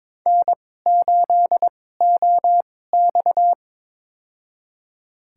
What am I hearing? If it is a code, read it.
Morse code, heard as a keyed tone, about 22 wpm, spaced as for 16 wpm, 700 Hz. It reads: N8OX
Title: USA Call Signs - 4 Characters - 2x Character Spacing - 22wpm